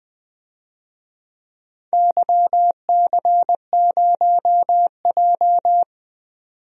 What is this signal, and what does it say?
Morse code, heard as a keyed tone, about 20 wpm, 700 Hz. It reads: YC0J
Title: DXCC-100 Call Signs - 4 Characters - 20wpm